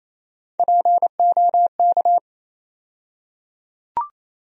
Morse code, heard as a keyed tone, about 27 wpm, 700 Hz. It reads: POK
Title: USA Call Sign Postfixes - 3 Letters 28wpm